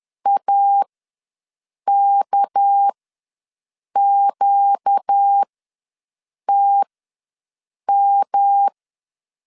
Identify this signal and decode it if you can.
Morse code, heard as a keyed tone, about 10 wpm, spaced as for 6 wpm, 800 Hz. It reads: AKQTM
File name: LEZIONI / 7 GF9Z / 7 G ws 10wpm 3843